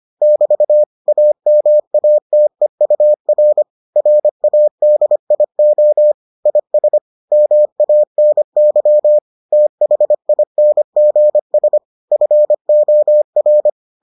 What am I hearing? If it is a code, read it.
Morse code, heard as a keyed tone, about 25 wpm, 600 Hz. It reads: <BT> AMATEUR RADIO IS MANY THINGS FOR